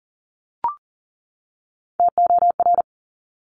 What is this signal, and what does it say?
Morse code, heard as a keyed tone, about 42 wpm, 700 Hz. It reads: TOR
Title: Top Three Letter Combinations in Words 40wpm